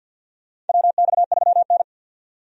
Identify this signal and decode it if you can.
Morse code, heard as a keyed tone, about 47 wpm, 700 Hz. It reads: WX3N
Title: USA Call Signs - 4 Characters - 50wpm